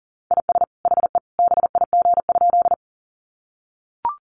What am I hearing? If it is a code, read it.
Morse code, heard as a keyed tone, about 40 wpm, 700 Hz. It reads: IS HE BIG?
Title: Sentences from Top 200 Words 40wpm